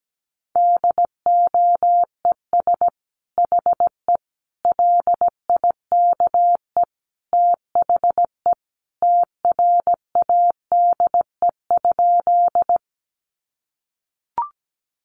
Morse code, heard as a keyed tone, about 17 wpm, 700 Hz. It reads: DOES HE LIKE THE TRADE?
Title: Sentences from Top 600 Words 17wpm